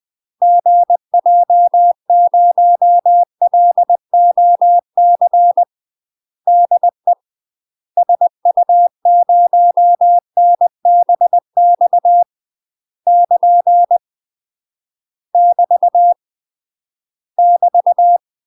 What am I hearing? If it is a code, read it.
Morse code, heard as a keyed tone, about 20 wpm, 700 Hz. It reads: GJ0LOC DE SU0NBX <KN> <BT> <BT>
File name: UK MASTER QSOs TRAINING / 20WPM / UK MASTER QSOs 0022